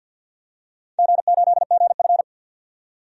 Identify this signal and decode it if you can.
Morse code, heard as a keyed tone, about 52 wpm, 700 Hz. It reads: G8GF
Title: DXCC-100 Call Signs - 4 Characters - 50wpm